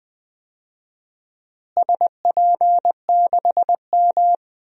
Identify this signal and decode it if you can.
Morse code, heard as a keyed tone, about 21 wpm, 700 Hz. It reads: SP6M
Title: DXCC-100 Call Signs - 4 Characters - Repeated 2x - 20wpm